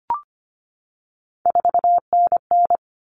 Morse code, heard as a keyed tone, about 25 wpm, 700 Hz. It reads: 4NN
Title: All Cut Numbers - 3-Digit 25wpm